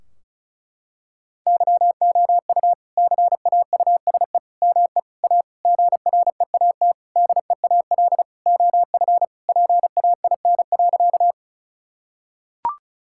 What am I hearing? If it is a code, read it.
Morse code, heard as a keyed tone, about 34 wpm, 700 Hz. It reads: YOU CAUSE ME A GREAT DEAL OF PAIN.
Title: Sentences from Top 1600 Words 35wpm